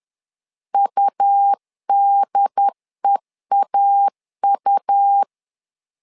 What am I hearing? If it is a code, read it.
Morse code, heard as a keyed tone, about 10 wpm, 800 Hz. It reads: UDEAU
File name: LEZIONI / 3 MWRO / 3 M 10wpm 1828